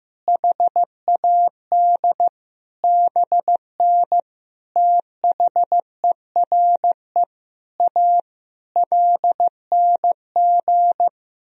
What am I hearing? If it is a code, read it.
Morse code, heard as a keyed tone, about 16 wpm, 700 Hz. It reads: HAD BN THERE A LNG
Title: Sentences from Top 400 Words and Abbreviations 15wpm